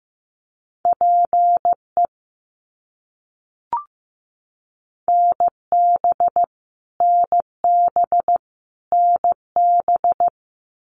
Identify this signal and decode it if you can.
Morse code, heard as a keyed tone, about 15 wpm, 700 Hz. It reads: PE NB NB NB
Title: Canadian Provinces and Territories - Repeated 3x - 15wpm